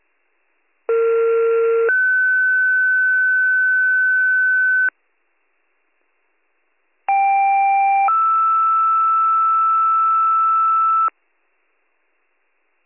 Two-tone page, heard as a one-second tone, then a three-second tone, about 460 then 1500 Hz, about 790 then 1320 Hz. It is heard from 0.9 to 11.1 s.